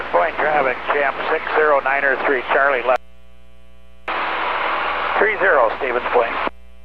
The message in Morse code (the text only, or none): none